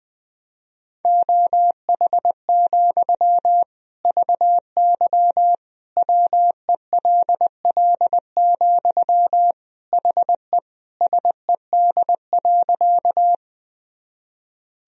Morse code, heard as a keyed tone, about 20 wpm, 700 Hz. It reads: OH, VY WELL, HE SED.
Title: Sentences from Top 500 Words and Abbreviations 20wpm